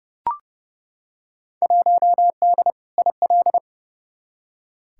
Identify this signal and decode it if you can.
Morse code, heard as a keyed tone, about 31 wpm, 700 Hz. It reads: 1D IL
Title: Field Day Exchanges - 30wpm